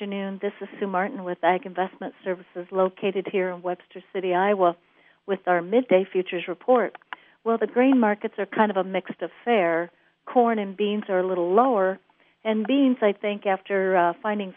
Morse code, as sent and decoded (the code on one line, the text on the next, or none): none